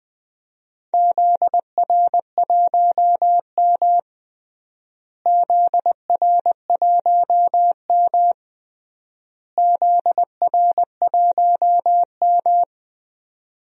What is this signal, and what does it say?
Morse code, heard as a keyed tone, about 20 wpm, 700 Hz. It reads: ZR1M ZR1M ZR1M
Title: DXCC-100 Call Signs - 4 Characters - Repeated 3x - 20wpm